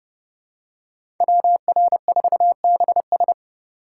Morse code, heard as a keyed tone, about 30 wpm, 700 Hz. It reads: WR4BS
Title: USA Call Signs - 5 Characters - 6x Character Spacing - 30wpm